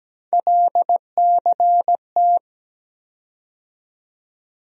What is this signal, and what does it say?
Morse code, heard as a keyed tone, about 17 wpm, 700 Hz. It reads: LCT
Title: USA Call Sign Postfixes - 3 Letters 17wpm